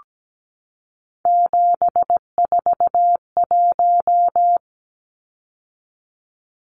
Morse code, heard as a keyed tone, about 17 wpm, 700 Hz. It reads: 741
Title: Numbers - 3-Digit 17wpm